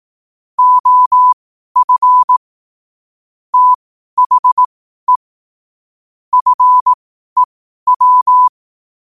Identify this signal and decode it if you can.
Morse code, heard as a keyed tone, about 18 wpm, spaced as for 12 wpm, 1000 Hz. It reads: OF THE FEW
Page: qotd-at-12-wpm